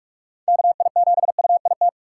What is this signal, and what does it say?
Morse code, heard as a keyed tone, about 45 wpm, 700 Hz. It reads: KI7UIT
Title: USA Call Signs - 6 Characters - 3x Character Spacing - 45wpm